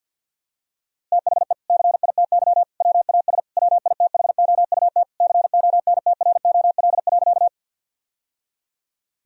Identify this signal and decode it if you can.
Morse code, heard as a keyed tone, about 50 wpm, 700 Hz. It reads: THE CITY WAS WITHOUT CONTROL.
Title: Sentences from Top 1000 Words - Review Part 3 - 50wpm